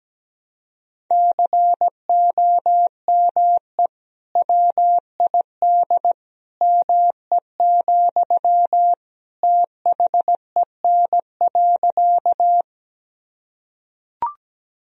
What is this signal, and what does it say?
Morse code, heard as a keyed tone, about 17 wpm, 700 Hz. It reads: COME WID ME, THEN.
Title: Sentences from Top 200 Words and Abbreviations 17wpm